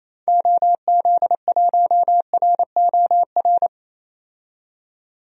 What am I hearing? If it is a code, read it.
Morse code, heard as a keyed tone, about 28 wpm, 700 Hz. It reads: OZ1ROR
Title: DXCC-100 Call Signs - 6 Characters - 28wpm